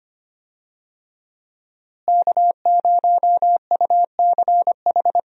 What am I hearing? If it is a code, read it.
Morse code, heard as a keyed tone, about 25 wpm, 700 Hz. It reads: K0UCH
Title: USA Call Signs - 5 Characters - Repeated 3x - 25wpm